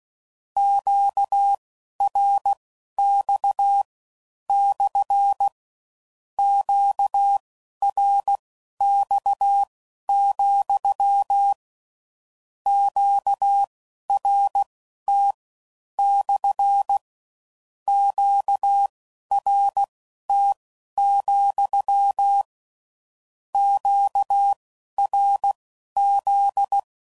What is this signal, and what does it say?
Morse code, heard as a keyed tone, about 16 wpm, spaced as for 11 wpm, 800 Hz. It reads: QRX / QRX, QRT / QRT, QRZ
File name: CW Code Groups @15WPM / Q Calls / Q Calls SPSL